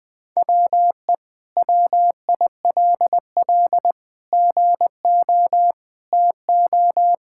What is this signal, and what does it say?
Morse code, heard as a keyed tone, about 20 wpm, 700 Hz. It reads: WE WILL GO TO
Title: Sentences with 2-Word Phrases - 20wpm